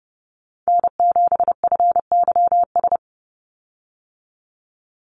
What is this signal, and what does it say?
Morse code, heard as a keyed tone, about 30 wpm, 700 Hz. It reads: N7FYS
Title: USA Call Signs 30wpm